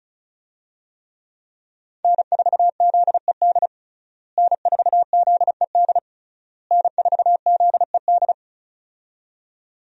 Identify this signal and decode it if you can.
Morse code, heard as a keyed tone, about 34 wpm, 700 Hz. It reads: N4ZED N4ZED N4ZED
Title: USA Call Signs - 5 Characters - Repeated 3x - 35wpm